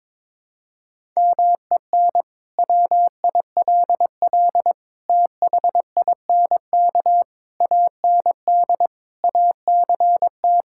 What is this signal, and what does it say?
Morse code, heard as a keyed tone, about 22 wpm, 700 Hz. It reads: MEN WILL THINK AND ACT
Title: Sentences from Top 1000 Words - Review Part 5 - 22wpm